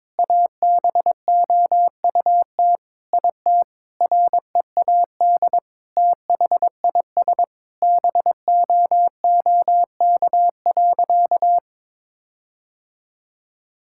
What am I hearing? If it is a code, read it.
Morse code, heard as a keyed tone, about 22 wpm, 700 Hz. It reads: ABOUT IT READ THIS BOOK.
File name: Sentences with 7-Word Phrases - 22wpm